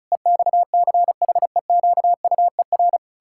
Morse code, heard as a keyed tone, about 36 wpm, 700 Hz. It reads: EXCHEQUER
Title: Pangram Challenge v2 - 35wpm